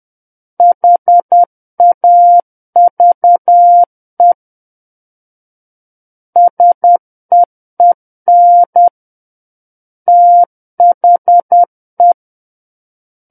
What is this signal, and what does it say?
Morse code, heard as a keyed tone, about 10 wpm, 700 Hz. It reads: HAVE SEEN THE